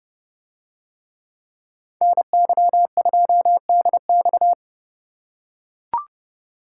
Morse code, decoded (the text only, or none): NY2DX